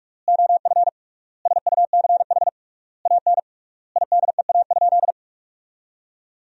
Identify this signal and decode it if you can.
Morse code, heard as a keyed tone, about 46 wpm, 700 Hz. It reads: OF SUCH AN IDEA?
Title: 3x Word-Spacing - Sentences from Top 400 Words - 45wpm